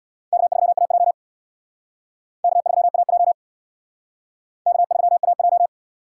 Morse code, heard as a keyed tone, about 60 wpm, 700 Hz. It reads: K3AJ K3AJ K3AJ
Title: Speed Racing - USA 1x2 Call Signs 50wpm